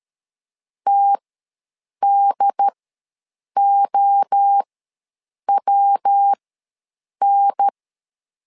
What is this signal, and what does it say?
Morse code, heard as a keyed tone, about 13 wpm, spaced as for 7 wpm, 800 Hz. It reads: TDOWN
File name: LEZIONI / 3 MWRO / 3 MWRO ws 12wpm 1914